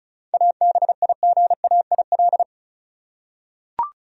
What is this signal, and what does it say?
Morse code, heard as a keyed tone, about 36 wpm, 700 Hz. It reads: ABIGAIL
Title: USA Common Names - 35wpm